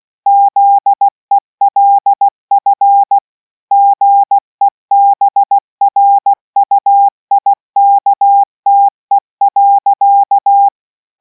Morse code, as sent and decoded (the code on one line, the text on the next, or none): --.. .-.. ..-. / --. -... .-. ..- .. -.- - . .-.-.-
ZELF GEBRUIKTE.